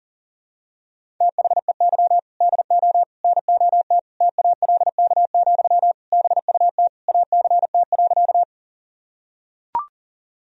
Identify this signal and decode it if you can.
Morse code, heard as a keyed tone, about 40 wpm, 700 Hz. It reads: THEY DO NOT TALK, BUT ACT.
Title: Sentences from Top 1000 Words - Review Part 1 - 40wpm